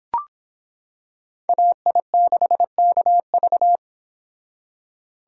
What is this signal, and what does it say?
Morse code, heard as a keyed tone, about 26 wpm, 700 Hz. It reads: AI6KV